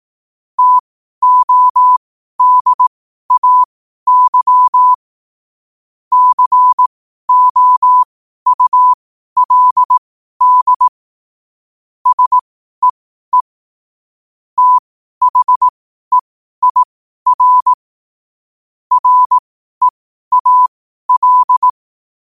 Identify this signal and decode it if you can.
Morse code, heard as a keyed tone, about 18 wpm, spaced as for 12 wpm, 1000 Hz. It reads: TODAY COULD SEE THEIR REAL
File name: qotdfiles / 12wpm